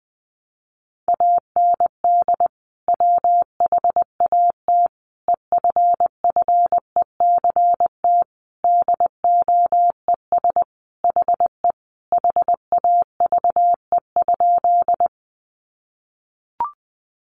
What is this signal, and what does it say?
Morse code, heard as a keyed tone, about 20 wpm, 700 Hz. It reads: AND WHAT EFFECT DOES HE HAVE?
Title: Sentences from Top 1000 Words 20wpm